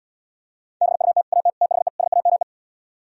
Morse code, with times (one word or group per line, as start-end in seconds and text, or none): 0.8-1.2 BUT
1.3-1.5 AT
1.6-1.9 THE
2.0-2.4 SAME